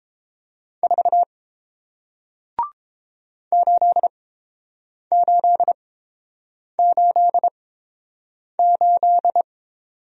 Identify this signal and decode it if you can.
Morse code, heard as a keyed tone, about 30 wpm, spaced as for 7 wpm, 700 Hz. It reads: 4 8888